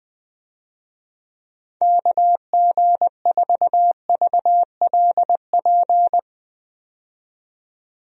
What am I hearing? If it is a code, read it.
Morse code, heard as a keyed tone, about 20 wpm, 700 Hz. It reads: KG4VLP
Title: USA Call Signs - 6 Characters - 20wpm